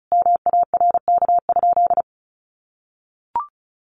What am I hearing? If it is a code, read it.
Morse code, heard as a keyed tone, about 34 wpm, 700 Hz. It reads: MARK?
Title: Sentences from Top 300 Words 35wpm